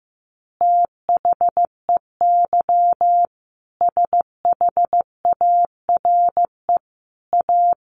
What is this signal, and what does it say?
Morse code, heard as a keyed tone, about 15 wpm, 700 Hz. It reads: THEY SHARE A
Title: Easy Phrases 15wpm